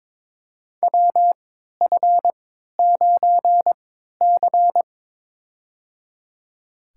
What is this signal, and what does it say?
Morse code, heard as a keyed tone, about 22 wpm, spaced as for 12 wpm, 700 Hz. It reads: WF9C
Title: USA Call Signs - 4 Characters - 3x Character Spacing - 22wpm